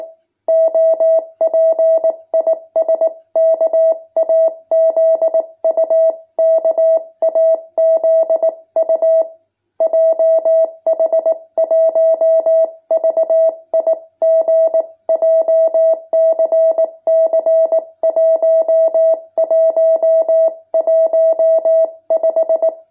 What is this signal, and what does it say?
Morse code, heard as a keyed tone, about 19 wpm, 650 Hz. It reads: OP IS KAZU KAZU JH1VIG JCC 1115
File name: jh8cbh / 2021cw